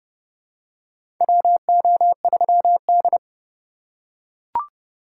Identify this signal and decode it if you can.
Morse code, heard as a keyed tone, about 30 wpm, 700 Hz. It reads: WO3D
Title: USA Call Signs - 4x Character Spacing - 30wpm